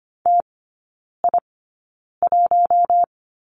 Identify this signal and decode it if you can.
Morse code, heard as a keyed tone, about 25 wpm, spaced as for 9 wpm, 700 Hz. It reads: TI1
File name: course / 25:10wpm / 093 Lesson 16e - Letter c - Three Characters 25wpm